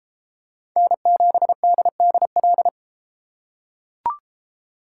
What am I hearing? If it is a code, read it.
Morse code, heard as a keyed tone, about 28 wpm, 700 Hz. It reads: N7DDL